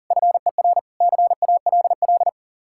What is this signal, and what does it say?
Morse code, heard as a keyed tone, about 48 wpm, 700 Hz. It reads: FER CALL
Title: Speed-Racing Common QSO Phrases - 40wpm